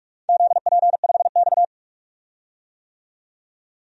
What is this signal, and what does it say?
Morse code, heard as a keyed tone, about 46 wpm, 700 Hz. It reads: ZP5X
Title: DXCC-100 Call Signs - 4 Characters - 45wpm